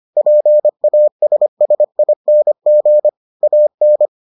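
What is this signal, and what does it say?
Morse code, heard as a keyed tone, about 25 wpm, 600 Hz. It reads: PASSING AN
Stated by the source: Morse edition of the Foundations of Amateur Radio episodes.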